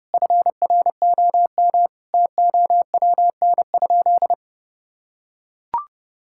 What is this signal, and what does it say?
Morse code, heard as a keyed tone, about 31 wpm, 700 Hz. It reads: FROM TOWN?
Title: Sentences from Top 1000 Words - Review Part 5 - 30wpm